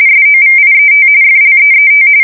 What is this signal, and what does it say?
RTTY, about 45.45 baud. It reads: SOVIET FLEET
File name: fsk170 / USS-New-Jersey